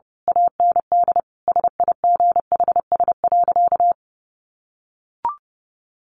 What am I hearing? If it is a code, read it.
Morse code, heard as a keyed tone, about 30 wpm, 700 Hz. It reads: AND SIGHS.
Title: A Sound Of Thunder 30wpm